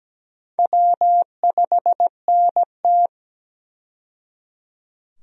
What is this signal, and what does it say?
Morse code, heard as a keyed tone, about 17 wpm, 700 Hz. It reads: W5NT
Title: USA Call Signs - 4 Characters - 17wpm